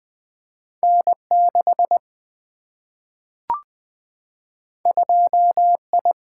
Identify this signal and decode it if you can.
Morse code, heard as a keyed tone, about 20 wpm, 700 Hz. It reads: N6 2I
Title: Single-Letter Single-Number 20wpm